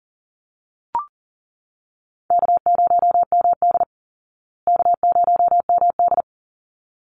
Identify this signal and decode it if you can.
Morse code, heard as a keyed tone, about 40 wpm, 700 Hz. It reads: K0MD K0MD
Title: Easy Call Signs Repeated 2x - 40wpm